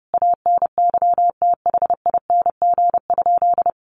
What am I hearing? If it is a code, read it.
Morse code, heard as a keyed tone, about 30 wpm, 700 Hz. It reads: ANYTHING?
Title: Sentences from Top 1500 Words 30wpm